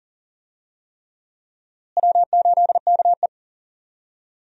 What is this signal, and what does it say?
Morse code, heard as a keyed tone, about 40 wpm, 700 Hz. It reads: W8KE
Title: USA Call Signs - 4 Characters - Repeated 3x - 40wpm